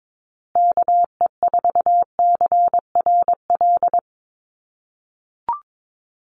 Morse code, heard as a keyed tone, about 22 wpm, 700 Hz. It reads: KE4CRL
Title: USA Call Signs Repeated 3x 22wpm